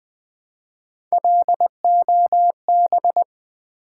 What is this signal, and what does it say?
Morse code, heard as a keyed tone, about 20 wpm, 700 Hz. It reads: LOB
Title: Build A Word - 1 Letter At A Time - 5-Letter Words - 20wpm